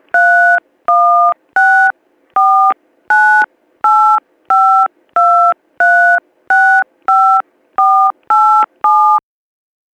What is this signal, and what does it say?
DTMF, heard as telephone touch tones, tones of about 370 ms. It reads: 31649852365487